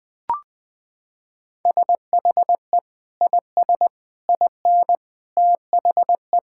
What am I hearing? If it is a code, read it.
Morse code, heard as a keyed tone, about 21 wpm, 700 Hz. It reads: SHE IS IN THE
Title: Sentences from Top 1600 Words 20wpm